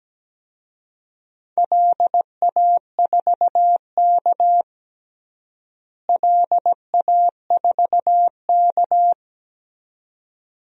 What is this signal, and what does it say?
Morse code, heard as a keyed tone, about 17 wpm, 700 Hz. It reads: LA4K LA4K
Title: DXCC-100 Call Signs - 4 Characters - Repeated 2x - 17wpm